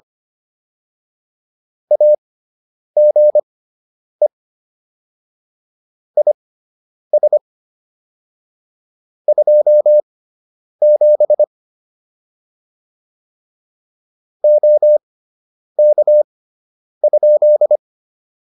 Morse code, heard as a keyed tone, about 25 wpm, spaced as for 9 wpm, 600 Hz. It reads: AGE IS 27 OK?
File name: QSO 202 09